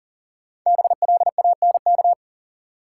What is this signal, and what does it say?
Morse code, heard as a keyed tone, about 40 wpm, 700 Hz. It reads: BLANK
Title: Sentences from Top 1500 Words - New Word Drill - 40wpm